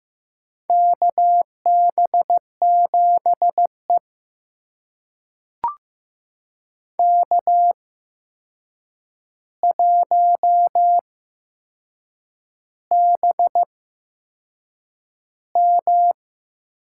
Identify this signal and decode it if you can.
Morse code, heard as a keyed tone, about 15 wpm, spaced as for 6 wpm, 700 Hz. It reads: KB7E K1BM
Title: USA Call Signs - 8x Character Spacing - 15wpm